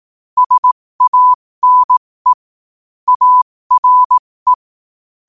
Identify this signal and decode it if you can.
Morse code, heard as a keyed tone, about 18 wpm, spaced as for 15 wpm, 1000 Hz. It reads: SANE ARE
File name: qotdfiles / 15wpm